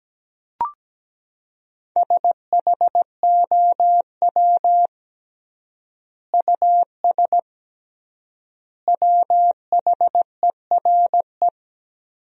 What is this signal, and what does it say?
Morse code, heard as a keyed tone, about 17 wpm, 700 Hz. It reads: SHOW US WHERE
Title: 3x Word-Spacing - Sentences from Top 200 Words - 17wpm